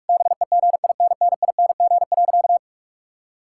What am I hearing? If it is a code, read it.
Morse code, heard as a keyed tone, about 46 wpm, 700 Hz. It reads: BEGINNING.
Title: Sentences from Top 1000 Words - Review Part 5 - 45wpm